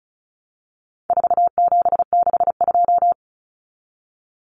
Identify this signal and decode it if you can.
Morse code, heard as a keyed tone, about 34 wpm, 700 Hz. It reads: 4762
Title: Numbers - 4-Digit 35wpm